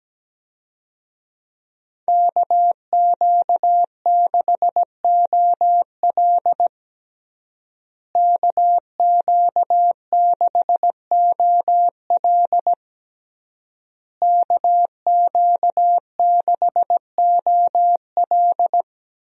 Morse code, heard as a keyed tone, about 17 wpm, 700 Hz. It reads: KQ6OL KQ6OL KQ6OL
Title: USA Call Signs - 5 Characters - Repeated 3x - 17wpm